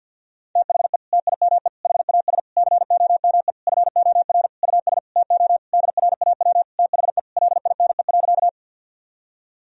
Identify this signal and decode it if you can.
Morse code, heard as a keyed tone, about 52 wpm, 700 Hz. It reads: THE TIME HAS COME FOR US TO DRAW THE LINE.
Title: Sentences from Top 1000 Words - Review Part 1 - 50wpm